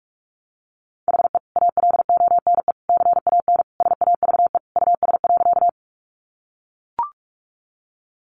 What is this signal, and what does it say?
Morse code, heard as a keyed tone, about 44 wpm, 700 Hz. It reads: HE ALONE CAN SAVE US.
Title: Sentences from Top 1100 Words 45wpm